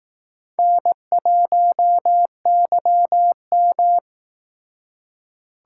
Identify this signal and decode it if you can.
Morse code, heard as a keyed tone, about 18 wpm, 700 Hz. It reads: N1YM
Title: Speed Racing - USA 1x2 Call Signs 15wpm